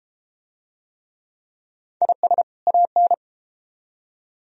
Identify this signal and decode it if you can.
Morse code, heard as a keyed tone, about 31 wpm, 700 Hz. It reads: IS AN